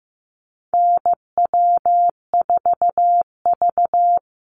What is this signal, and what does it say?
Morse code, heard as a keyed tone, about 15 wpm, 700 Hz. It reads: NW4V
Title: Call Signs Repeated 3x - 15wpm